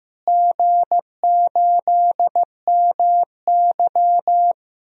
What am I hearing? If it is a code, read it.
Morse code, heard as a keyed tone, about 15 wpm, 700 Hz. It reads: G8MY
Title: DXCC-100 Call Signs - 4 Characters - Repeated 3x - 15wpm